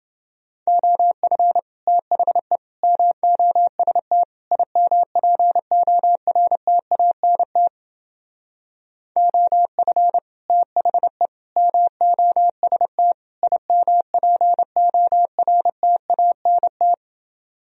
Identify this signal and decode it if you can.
Morse code, heard as a keyed tone, about 30 wpm, 700 Hz. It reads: OF THE MOST IMPORTANT OF THE MOST IMPORTANT